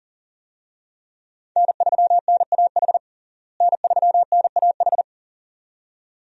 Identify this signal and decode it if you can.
Morse code, heard as a keyed tone, about 37 wpm, 700 Hz. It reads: N3NAH N3NAH
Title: USA Call Signs - 5 Characters - Repeated 2x - 40wpm